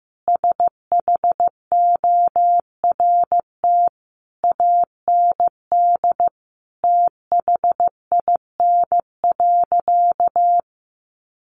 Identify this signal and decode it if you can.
Morse code, heard as a keyed tone, about 15 wpm, 700 Hz. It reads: SHORT AND THIN.
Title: Sentences from Top 1200 Words 15wpm